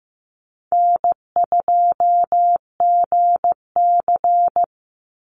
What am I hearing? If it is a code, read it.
Morse code, heard as a keyed tone, about 15 wpm, 700 Hz. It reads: N2GC
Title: Easy Call Signs - 15wpm